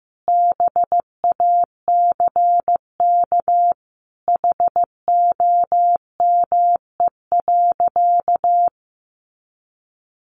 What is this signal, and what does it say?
Morse code, heard as a keyed tone, about 15 wpm, 700 Hz. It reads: BACK HOME.
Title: Sentences from Top 500 Words 15wpm